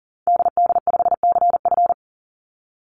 Morse code, heard as a keyed tone, about 40 wpm, 700 Hz. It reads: DD5CF
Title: Call Signs - 40wpm